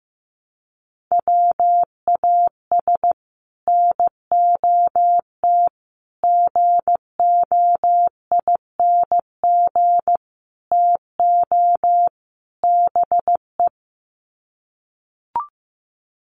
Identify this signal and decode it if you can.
Morse code, heard as a keyed tone, about 15 wpm, 700 Hz. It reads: WAS NOT GOING TO BE
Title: Sets of 5 Words Encore 15wpm